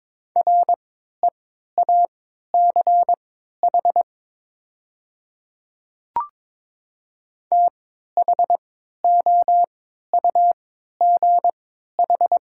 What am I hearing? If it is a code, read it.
Morse code, heard as a keyed tone, about 22 wpm, spaced as for 12 wpm, 700 Hz. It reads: REACH THOUGH
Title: 3x Character Spacing - Top 500 Words - 22wpm